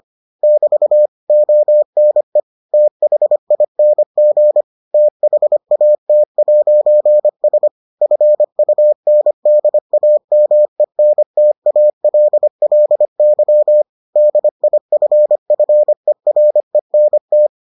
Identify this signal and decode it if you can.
Morse code, heard as a keyed tone, about 25 wpm, 600 Hz. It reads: <BT> ONE THING THAT'S FUNDAMENTALLY DIFFERENT